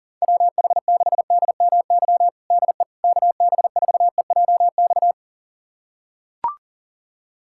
Morse code, heard as a keyed tone, about 40 wpm, 700 Hz. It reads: WH6DMY DE KB4EJX
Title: USA Callsign DE Callsign - 6 Characters - 40wpm